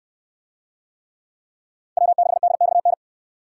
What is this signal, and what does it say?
Morse code, heard as a keyed tone, about 68 wpm, 700 Hz. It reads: W6DXN